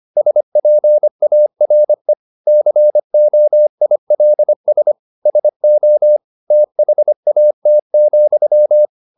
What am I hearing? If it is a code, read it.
Morse code, heard as a keyed tone, about 25 wpm, 600 Hz. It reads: SPARE COILS SO THAT,
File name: the old man and the sea part 1 25wpm